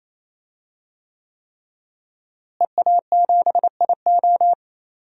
Morse code, heard as a keyed tone, about 27 wpm, 700 Hz. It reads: EA7IO
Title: DXCC-100 Call Signs - 5 Characters - Repeated 3x - 28wpm